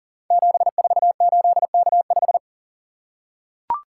Morse code, heard as a keyed tone, about 41 wpm, 700 Hz. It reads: 748K5
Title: Code Groups - 5 Letter:Numbers - 40wpm